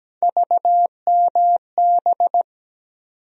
Morse code, heard as a keyed tone, about 17 wpm, 700 Hz. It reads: VMB
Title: USA Call Sign Postfixes - 3 Letters 17wpm